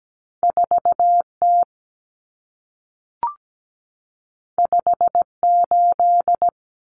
Morse code, heard as a keyed tone, about 17 wpm, 700 Hz. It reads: 4T 58
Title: Common Cut Numbers - 2-Digit 17wpm